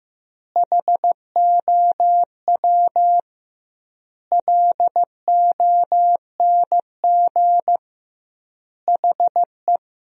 Morse code, heard as a keyed tone, about 15 wpm, 700 Hz. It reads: HOW LONG HE
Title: 2x Word-Spacing - Sentences from Top 300 Words - 15wpm